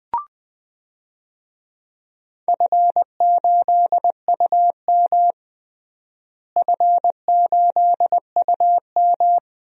Morse code, heard as a keyed tone, about 21 wpm, 700 Hz. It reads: F8UM F8UM
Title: DXCC-100 Call Signs - 4 Characters - Repeated 2x - 20wpm